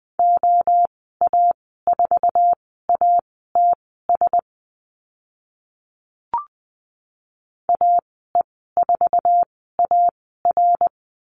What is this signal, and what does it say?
Morse code, heard as a keyed tone, about 20 wpm, spaced as for 15 wpm, 700 Hz. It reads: OA4ATS AE4AR
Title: 082 Lesson 14g - Number 2 - Call Signs 20wpm